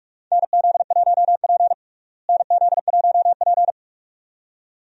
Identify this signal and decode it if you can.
Morse code, heard as a keyed tone, about 46 wpm, 700 Hz. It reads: NZ1P NZ1P
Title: USA Call Signs - 4 Characters - Repeated 2x - 45wpm